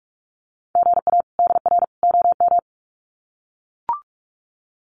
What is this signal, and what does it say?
Morse code, heard as a keyed tone, about 44 wpm, 700 Hz. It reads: GA DR OM
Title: Common QSO Phrases 45wpm